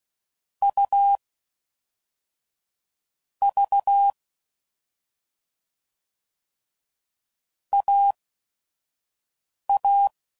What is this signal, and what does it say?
Morse code, heard as a keyed tone, about 16 wpm, spaced as for 5 wpm, 800 Hz. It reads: U V AA